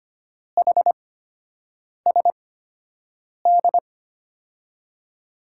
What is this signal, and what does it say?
Morse code, heard as a keyed tone, about 25 wpm, spaced as for 7 wpm, 700 Hz. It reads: HSD
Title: USA Call Signs - 8x Character Spacing - 25wpm